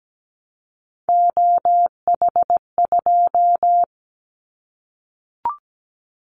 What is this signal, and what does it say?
Morse code, heard as a keyed tone, about 17 wpm, 700 Hz. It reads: OH2
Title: Callsign Prefixes for Top 50 DXCC - 17wpm